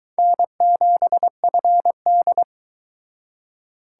Morse code, heard as a keyed tone, about 22 wpm, 700 Hz. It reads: N7FD